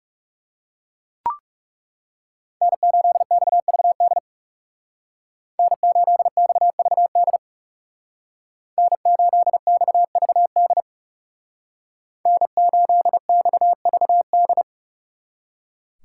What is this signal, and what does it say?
Morse code, heard as a keyed tone, about 39 wpm, 700 Hz. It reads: N8XVD N8XVD N8XVD N8XVD